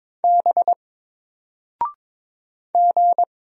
Morse code, heard as a keyed tone, about 22 wpm, 700 Hz. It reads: B G
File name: Single-Letter-Number-22wpm